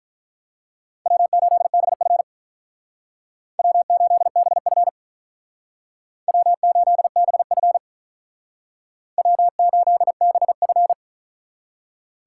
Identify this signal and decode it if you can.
Morse code, heard as a keyed tone, about 45 wpm, 700 Hz. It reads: W8BF W8BF W8BF W8BF